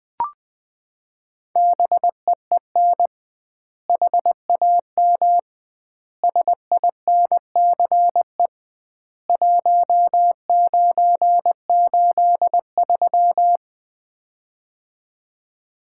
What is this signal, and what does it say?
Morse code, heard as a keyed tone, about 20 wpm, 700 Hz. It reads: BEEN HAM SINCE 1983
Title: Common QSO Phrases - 2x Word-Spacing - 20wpm